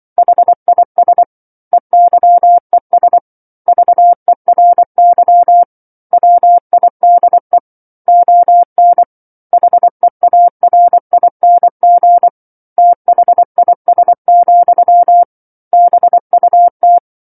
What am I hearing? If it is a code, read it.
Morse code, heard as a keyed tone, about 24 wpm, 700 Hz. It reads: HIS EYES VERY WIDE ON HEARING THIS, BUT
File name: 07-Chapter-07 24 WPM